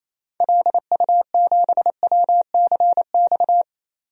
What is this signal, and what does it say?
Morse code, heard as a keyed tone, about 28 wpm, 700 Hz. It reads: LU7WCX
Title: DXCC-100 Call Signs - 6 Characters - 28wpm